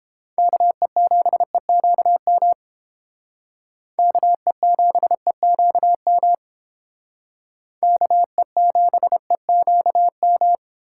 Morse code, heard as a keyed tone, about 29 wpm, 700 Hz. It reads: KE7EQM KE7EQM KE7EQM